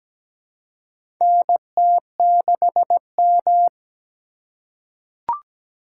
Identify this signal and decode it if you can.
Morse code, heard as a keyed tone, about 17 wpm, 700 Hz. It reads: NT6M
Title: USA Call Signs - 3x Character Spacing - 17wpm